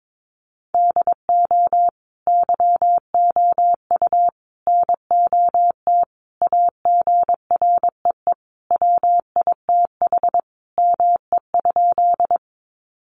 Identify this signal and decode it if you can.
Morse code, heard as a keyed tone, about 22 wpm, 700 Hz. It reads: DO YOU NOT AGREE WITH ME?
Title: Sentences from Top 1100 Words 22wpm